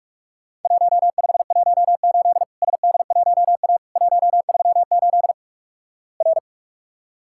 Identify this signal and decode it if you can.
Morse code, heard as a keyed tone, about 46 wpm, 700 Hz. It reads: 1518 SD1A 138 R
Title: Worked All Europe DX Contest - QTC Exchange - 45wpm